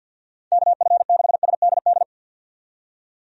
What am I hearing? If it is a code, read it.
Morse code, heard as a keyed tone, about 52 wpm, 700 Hz. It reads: KF6SDD
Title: USA Call Signs - 6 Characters - 50wpm